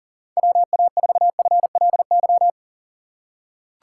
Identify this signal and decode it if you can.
Morse code, heard as a keyed tone, about 40 wpm, 700 Hz. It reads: WA4FLY
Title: USA Call Signs - 6 Characters - 2x Character Spacing - 40wpm